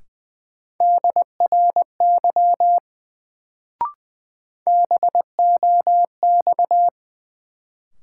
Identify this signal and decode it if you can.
Morse code, heard as a keyed tone, about 20 wpm, 700 Hz. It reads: DRY BOX
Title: Sentences from Top 500 Words - New Word Drill - 20wpm